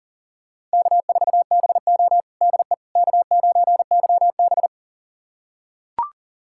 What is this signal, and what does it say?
Morse code, heard as a keyed tone, about 40 wpm, 700 Hz. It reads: K4BO DE K9YB
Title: USA Callsign DE Callsign - 4 Characters - 40wpm